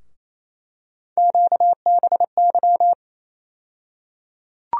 Morse code, heard as a keyed tone, about 28 wpm, 700 Hz. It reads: QBY
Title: USA Call Sign Postfixes - 3 Letters 28wpm